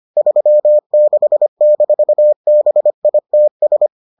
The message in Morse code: 36-BITS

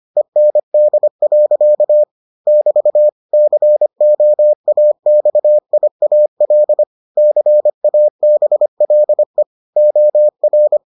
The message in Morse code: END. <BT> COAXIAL CABLE OR

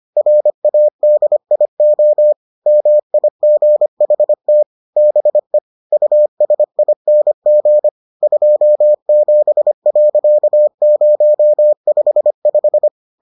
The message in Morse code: RADIO MIGHT BE USING 27.055

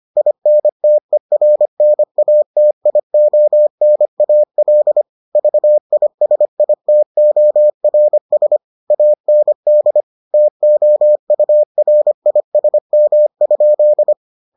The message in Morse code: INTERNATIONAL VISITORS AND TOURISM?

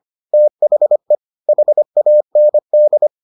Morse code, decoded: THE HAND

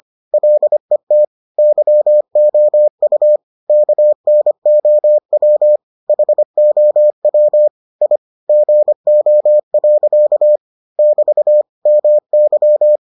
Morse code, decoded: LET YOU KNOW HOW I GO. <BT> MY

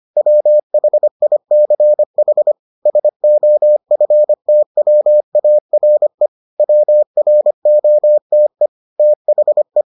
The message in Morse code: WHICH SOFTWARE WROTE THE

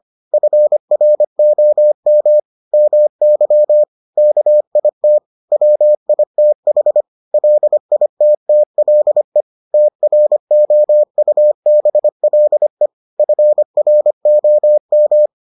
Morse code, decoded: FROM MY KIT WITH LITTLE TROUBLE FROM

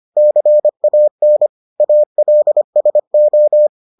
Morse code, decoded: CAN ALSO

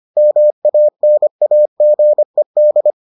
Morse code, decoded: MANAGED

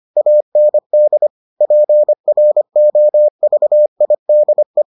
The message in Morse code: AND PROVIDE

